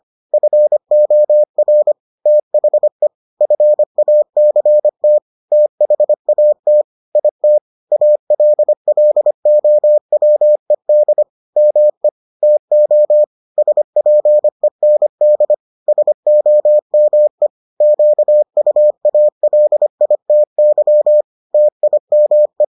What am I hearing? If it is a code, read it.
Morse code, heard as a keyed tone, about 25 wpm, 600 Hz. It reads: FOR THE FACT THAT IT ALLOWED ME TO SPEND SOME QUALITY TIME